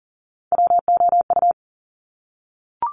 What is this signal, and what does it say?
Morse code, heard as a keyed tone, about 38 wpm, 690 Hz. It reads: WOU
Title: Top Three Letter Combinations in Words 40wpm